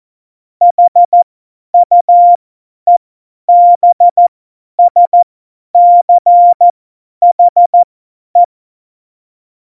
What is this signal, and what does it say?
Morse code, heard as a keyed tone, about 14 wpm, spaced as for 10 wpm, 700 Hz. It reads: HUEBSCHE
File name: Morsekurs / Witze / Witz 01. Dezember 2020, 50 F70